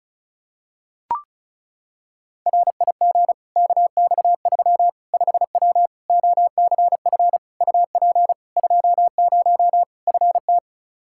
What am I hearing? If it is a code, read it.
Morse code, heard as a keyed tone, about 36 wpm, 700 Hz. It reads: RIG KX3 5W OCF UP 20 FT
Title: Practice QSO - Exchange 2 - Info - 35wpm